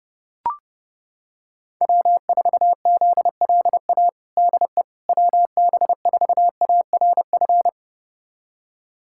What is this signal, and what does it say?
Morse code, heard as a keyed tone, about 30 wpm, 700 Hz. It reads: W4ZLA DE WB4ARF
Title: USA Callsign DE Callsign 30wpm